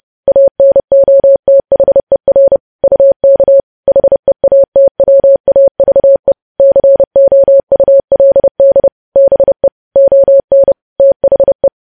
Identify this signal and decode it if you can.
Morse code, heard as a keyed tone, about 30 wpm, 550 Hz. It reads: ANOTHER UK HEATWAVE COULD BE ON THE
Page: news-headlines-in-morse-code-at-30-wpm